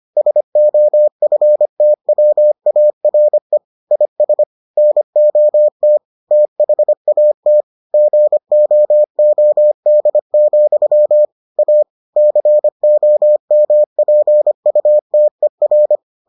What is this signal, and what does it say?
Morse code, heard as a keyed tone, about 25 wpm, 600 Hz. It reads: SOFTWARE IS NOT THAT GOOD, A COMPUTER